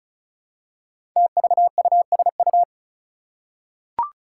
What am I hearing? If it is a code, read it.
Morse code, heard as a keyed tone, about 36 wpm, 700 Hz. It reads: TVUSU
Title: Five Challenging Letter-Number Combinations 35wpm